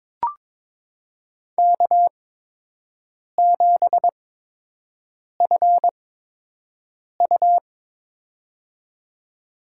Morse code, heard as a keyed tone, about 22 wpm, spaced as for 6 wpm, 700 Hz. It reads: K7FU
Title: USA Call Signs - 4 Characters - 8x Character Spacing - 22wpm